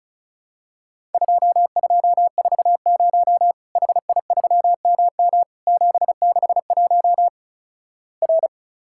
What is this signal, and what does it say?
Morse code, heard as a keyed tone, about 34 wpm, 700 Hz. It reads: 2240 HI3MM 761 R
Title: Worked All Europe DX Contest - QTC Exchange - 35wpm